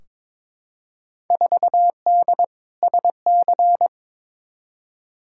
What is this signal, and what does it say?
Morse code, heard as a keyed tone, about 22 wpm, 700 Hz. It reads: 4D SC
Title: Field Day Exchanges Encore - 22wpm